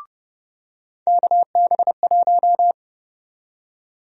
Morse code, heard as a keyed tone, about 32 wpm, 700 Hz. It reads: KB1
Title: USA Call Sign Prefixes - 30wpm